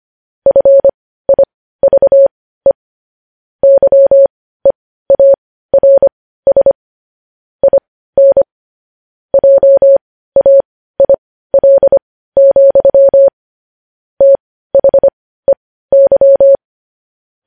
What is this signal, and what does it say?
Morse code, heard as a keyed tone, about 25 wpm, spaced as for 15 wpm, 550 Hz. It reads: FIVE YEARS IN JAIL, THEY
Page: news headlines in morse code at 15 wpm